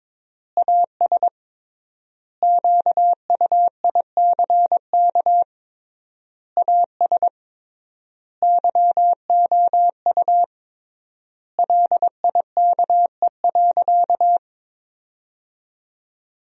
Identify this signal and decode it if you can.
Morse code, heard as a keyed tone, about 22 wpm, 700 Hz. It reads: AS QUICK AS YOU LIKE.
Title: 3x Word-Spacing - Sentences from Top 500 Words - 22wpm